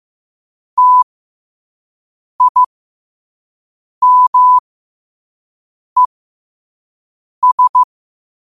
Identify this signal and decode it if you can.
Morse code, heard as a keyed tone, about 15 wpm, spaced as for 5 wpm, 1000 Hz. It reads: TIMES
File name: qotdfiles / 5wpm